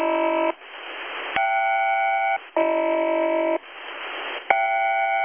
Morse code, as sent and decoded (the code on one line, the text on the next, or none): none